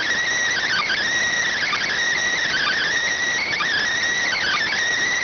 Emergency Alert System (SAME) tones, heard as harsh buzzing bursts. It is heard from 0.6 to 4.6 s.